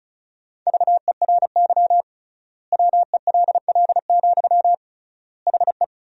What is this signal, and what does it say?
Morse code, heard as a keyed tone, about 36 wpm, 700 Hz. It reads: VERY WELL, HE
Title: 3x Word-Spacing - Sentences from Top 200 Words - 35wpm